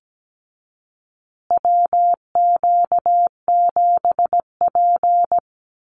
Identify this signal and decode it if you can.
Morse code, heard as a keyed tone, about 17 wpm, 700 Hz. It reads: WQ7P
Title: USA Call Signs Repeated 2x 17wpm